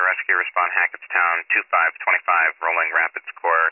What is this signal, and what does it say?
Emergency Alert System (SAME) tones, heard as harsh buzzing bursts, with speech. It is heard from 0.7 to 3.3 s.